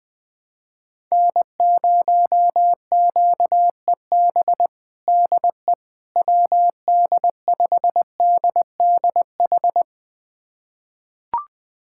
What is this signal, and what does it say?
Morse code, heard as a keyed tone, about 20 wpm, 700 Hz. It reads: N0QEB DE WD5DDH
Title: USA Callsign DE Callsign 20wpm